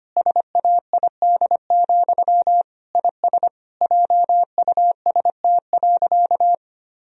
Morse code, heard as a keyed tone, about 25 wpm, 700 Hz. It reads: SAID, IS JUST.
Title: Sentences from Top 1000 Words - Review Part 4 - 25wpm